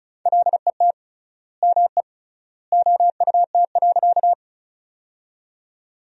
Morse code, heard as a keyed tone, about 34 wpm, 700 Hz. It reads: LET ME OUT.
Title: 3x Word-Spacing - Sentences from Top 300 Words - 35wpm